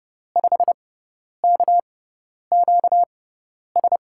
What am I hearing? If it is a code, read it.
Morse code, heard as a keyed tone, about 30 wpm, spaced as for 10 wpm, 700 Hz. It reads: 5KQS
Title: USA Call Signs - 6 Characters - 6x Character Spacing - 30wpm